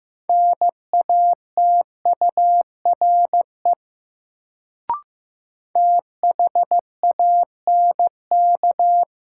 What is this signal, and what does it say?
Morse code, heard as a keyed tone, about 15 wpm, 700 Hz. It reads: NATURE THANK
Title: Sentences from Top 600 Words - New Word Drill - 15wpm